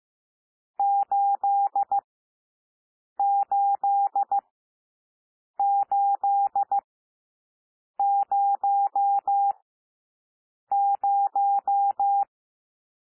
Morse code, heard as a keyed tone, about 15 wpm, spaced as for 6 wpm, 800 Hz. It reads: 88800
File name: cwmp3 / N1 SOUND NUMBERS 15 wpm